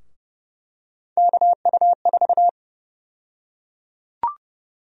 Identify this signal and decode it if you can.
Morse code, heard as a keyed tone, about 31 wpm, 700 Hz. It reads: KU4
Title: USA Call Sign Prefixes - Repeated 2x - 30wpm